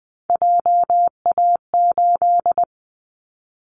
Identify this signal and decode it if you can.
Morse code, heard as a keyed tone, about 20 wpm, 700 Hz. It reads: JA8
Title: Callsign Prefixes for Top 25 DXCC - 20wpm